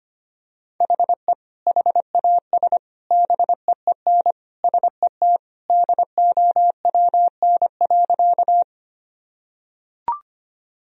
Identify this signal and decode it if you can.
Morse code, heard as a keyed tone, about 25 wpm, 700 Hz. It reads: HE HAS BEEN SET DOWN.
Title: Sentences from Top 1000 Words - Review Part 1 - 25wpm